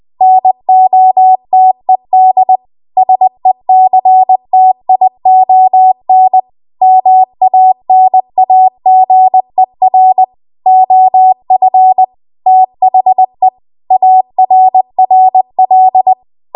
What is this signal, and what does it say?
Morse code, heard as a keyed tone, about 20 wpm, 750 Hz. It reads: NOTED SECTION MANAGER OF THE ARRL
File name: Morse / 20WPM